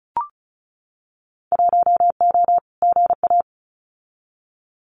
Morse code, heard as a keyed tone, about 34 wpm, 700 Hz. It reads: 1O GA
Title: Winter Field Day Exchanges 35wpm